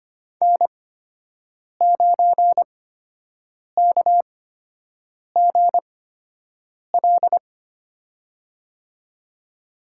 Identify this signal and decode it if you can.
Morse code, heard as a keyed tone, about 25 wpm, spaced as for 7 wpm, 700 Hz. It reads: N9KGL
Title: USA Call Signs - 5 Characters - 8x Character Spacing - 25wpm